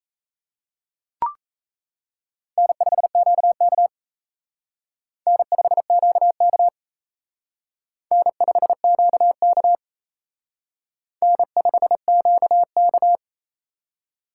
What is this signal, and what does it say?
Morse code, heard as a keyed tone, about 35 wpm, 700 Hz. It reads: N5QK N5QK N5QK N5QK